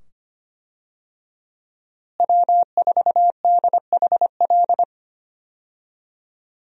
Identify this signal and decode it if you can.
Morse code, heard as a keyed tone, about 25 wpm, 690 Hz. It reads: W4DHL
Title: USA Call Signs - 5 Characters - Repeated 2x - 25wpm